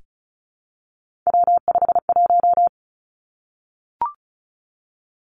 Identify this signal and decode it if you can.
Morse code, heard as a keyed tone, about 36 wpm, 700 Hz. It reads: W51
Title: Three Challenging Letter-Number Combinations 35wpm